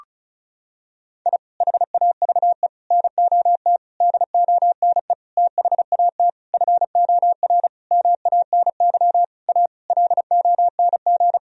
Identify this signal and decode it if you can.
Morse code, heard as a keyed tone, about 34 wpm, 700 Hz. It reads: I HAVE NOT DONE THAT FOR MANY A LONG
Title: Sentences from Top 1000 Words - Review Part 1 - 35wpm